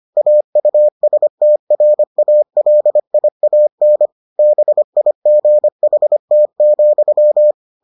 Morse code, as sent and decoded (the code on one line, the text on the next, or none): .- ..- ... - .-. .- .-.. .. .- -. / -... .. --. .... - --..--
AUSTRALIAN BIGHT,